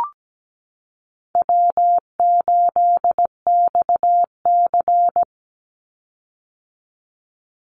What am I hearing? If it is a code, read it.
Morse code, heard as a keyed tone, about 17 wpm, 700 Hz. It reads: W8XC
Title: Easy Call Signs - 17wpm